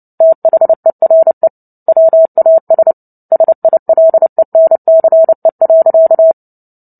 Morse code, heard as a keyed tone, about 30 wpm, 650 Hz. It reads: THERE WAS SILENCE.
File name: Book Two / 03-Chapter 3 - 29 WPM